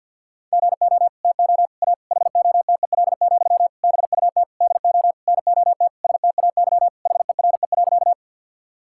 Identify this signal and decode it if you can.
Morse code, heard as a keyed tone, about 50 wpm, 700 Hz. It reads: GO TO A HOTEL, BUT DO NOT STAY HERE.
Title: Sentences from Top 1600 Words 50wpm